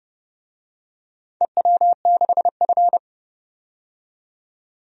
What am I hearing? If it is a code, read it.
Morse code, heard as a keyed tone, about 30 wpm, 700 Hz. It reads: EW6F
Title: DXCC-100 Call Signs - 4 Characters - 30wpm